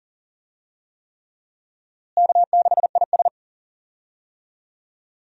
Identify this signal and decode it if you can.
Morse code, heard as a keyed tone, about 40 wpm, 700 Hz. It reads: K6IS
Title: USA Call Signs - 4 Characters - Repeated 3x - 40wpm